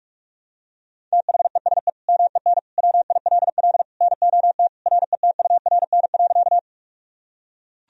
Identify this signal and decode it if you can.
Morse code, heard as a keyed tone, about 45 wpm, 700 Hz. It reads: THESE MEN WILL NOT RETURN.
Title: Sentences from Top 1000 Words - Review Part 1 - 45wpm